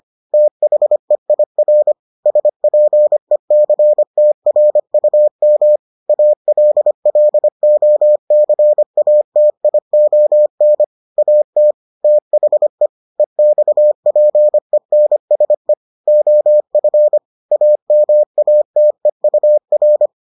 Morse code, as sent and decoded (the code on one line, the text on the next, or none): - .... .. .-. / ... .--. -.-. - .-. ..- -- / .- .-.. .-.. --- -.-. .- - .. --- -. / .- - / - .... / . -..- .--. -. ... / --- ..-. / .- -- .- - . ..- .-.
THEIR SPECTRUM ALLOCATION AT THE EXPENSE OF AMATEUR